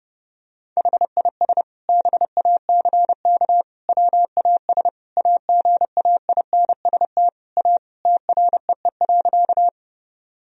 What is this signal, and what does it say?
Morse code, heard as a keyed tone, about 30 wpm, 700 Hz. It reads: HIS BACK WAS AGAINST A TREE.
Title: Sentences from Top 1000 Words - Review Part 1 - 30wpm